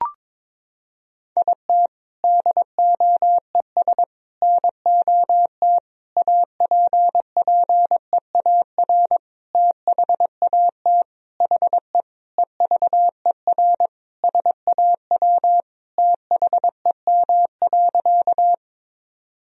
Morse code, heard as a keyed tone, about 21 wpm, 700 Hz. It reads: IT DOES NOT APPEAR THAT HE EVER SAW THEM.
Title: Sentences from Top 1000 Words - Review Part 2 - 22wpm